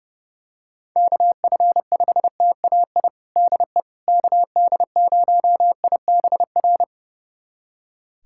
Morse code, heard as a keyed tone, about 30 wpm, 700 Hz. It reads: KF5TAI DE KD0IBR
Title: USA Callsign DE Callsign 30wpm